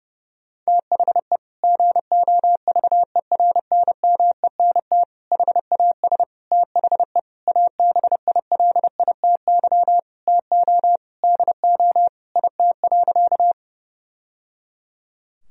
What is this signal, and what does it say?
Morse code, heard as a keyed tone, about 32 wpm, 700 Hz. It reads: THE GOVERNMENT HAS THE ABILITY TO DO IT.
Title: Sentences from Top 1000 Words - Review Part 3 - 30wpm